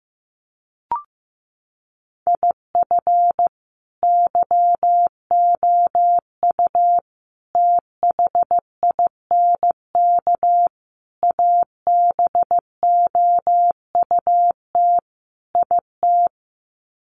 Morse code, heard as a keyed tone, about 15 wpm, 700 Hz. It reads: IF YOU THINK ABOUT IT
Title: Sets of 5 Words Encore2 15wpm